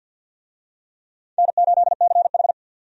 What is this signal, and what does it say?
Morse code, heard as a keyed tone, about 51 wpm, 700 Hz. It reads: N8CH
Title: USA Call Signs - 4 Characters - 50wpm